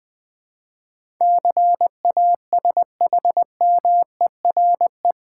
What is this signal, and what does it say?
Morse code, heard as a keyed tone, about 20 wpm, 700 Hz. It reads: CASHMERE
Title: Difficult Words with Hints - 20wpm